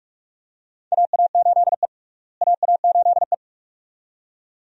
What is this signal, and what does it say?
Morse code, heard as a keyed tone, about 44 wpm, 700 Hz. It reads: AA8E AA8E
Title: USA Call Signs - 4 Characters - Repeated 2x - 45wpm